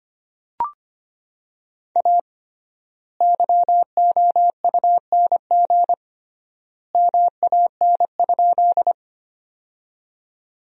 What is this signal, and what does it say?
Morse code, heard as a keyed tone, about 25 wpm, 700 Hz. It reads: A YOUNG MAN?
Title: 3x Word-Spacing - Sentences from Top 400 Words - 25wpm